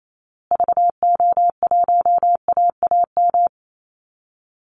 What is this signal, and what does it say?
Morse code, heard as a keyed tone, about 28 wpm, 700 Hz. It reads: VO1AAM
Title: Call Signs Repeated 2x - 28wpm